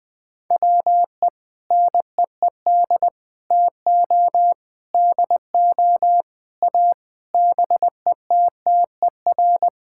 Morse code, heard as a keyed tone, about 20 wpm, 700 Hz. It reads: WE NEED TO DO A BETTER
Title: Sentences with 7-Word Phrases - Spoken Beginning - 20wpm